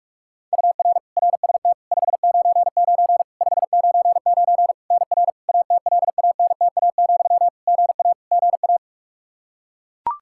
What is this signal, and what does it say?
Morse code, heard as a keyed tone, about 42 wpm, 700 Hz. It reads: UR RST 599 599 NR ATLANTA, GA GA
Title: Practice QSO - Exchange 1 - Beginning - 45wpm